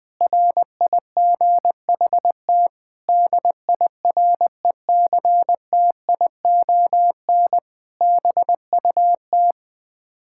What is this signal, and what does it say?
Morse code, heard as a keyed tone, about 20 wpm, 700 Hz. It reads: RIGHT DIRECTION BUT
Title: Sets of 8 Words - 20wpm